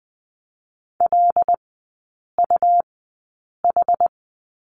Morse code, heard as a keyed tone, about 20 wpm, spaced as for 8 wpm, 700 Hz. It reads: LUH
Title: 184 Lesson 30e - Letter b - Three Characters 20wpm